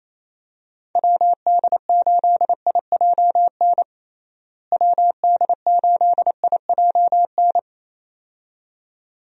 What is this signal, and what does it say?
Morse code, heard as a keyed tone, about 28 wpm, 700 Hz. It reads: WD8IJN WD8IJN
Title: USA Call Signs - 6 Characters - Repeated 2x - 28wpm